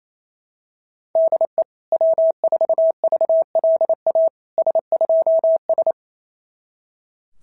Morse code, heard as a keyed tone, about 27 wpm, 690 Hz. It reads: DE W4VLA S2S
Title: SOTA QSO Practice 28wpm